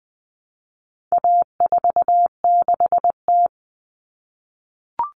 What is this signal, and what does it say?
Morse code, heard as a keyed tone, about 19 wpm, 700 Hz. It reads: A46T
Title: Common Cut Numbers - 4-Digit 20wpm